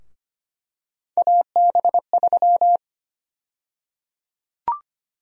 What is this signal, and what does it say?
Morse code, heard as a keyed tone, about 25 wpm, 700 Hz. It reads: AB3
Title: USA Call Sign Prefixes - Repeated 2x - 25wpm